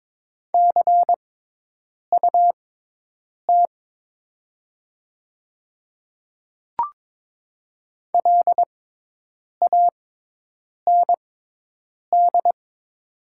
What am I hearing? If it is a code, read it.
Morse code, heard as a keyed tone, about 22 wpm, spaced as for 8 wpm, 700 Hz. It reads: CUT LAND
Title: 6x Character Spacing - Top 300 Words - 22wpm